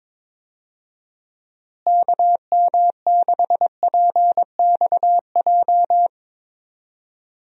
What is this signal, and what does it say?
Morse code, heard as a keyed tone, about 22 wpm, 700 Hz. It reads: KM6PXJ
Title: USA Call Signs - 6 Characters - Repeated 3x - 22wpm